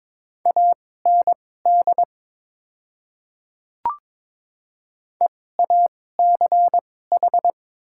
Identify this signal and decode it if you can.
Morse code, heard as a keyed tone, about 22 wpm, spaced as for 16 wpm, 700 Hz. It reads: AND EACH
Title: 2x Character Spacing - Top 100 Words - 22wpm